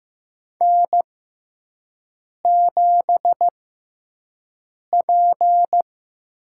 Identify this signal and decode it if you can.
Morse code, heard as a keyed tone, about 15 wpm, spaced as for 5 wpm, 700 Hz. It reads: N7P
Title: USA Call Signs - 6x Character Spacing - 15wpm